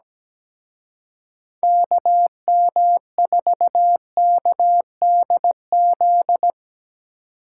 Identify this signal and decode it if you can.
Morse code, heard as a keyed tone, about 19 wpm, 700 Hz. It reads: KM4KDZ
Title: Speed Racing - USA 2x3 Call Signs 17wpm